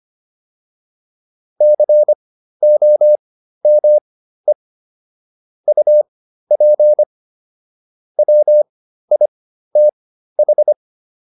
Morse code, heard as a keyed tone, about 25 wpm, spaced as for 13 wpm, 600 Hz. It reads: COME UP WITH